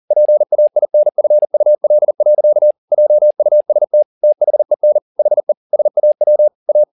Morse code, heard as a keyed tone, about 40 wpm, 600 Hz. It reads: PAINFUL. JUST THEN HE SAW A